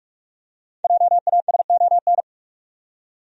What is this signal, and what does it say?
Morse code, heard as a keyed tone, about 46 wpm, 700 Hz. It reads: JASON
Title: USA Common Names - 45wpm